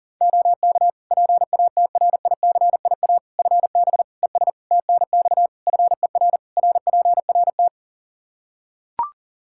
Morse code, heard as a keyed tone, about 40 wpm, 700 Hz. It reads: OK PATRICIA FB ES TNX FER RPRT
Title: Practice QSO - Exchange 2 - Info - 40wpm